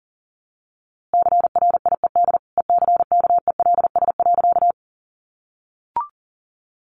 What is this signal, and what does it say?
Morse code, heard as a keyed tone, about 40 wpm, 700 Hz. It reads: CRIED ECKELS.
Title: A Sound Of Thunder 40wpm